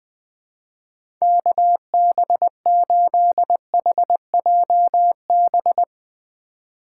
Morse code, heard as a keyed tone, about 20 wpm, 700 Hz. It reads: KB8HJB
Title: USA Call Signs - 6 Characters - 6x Character Spacing - 20wpm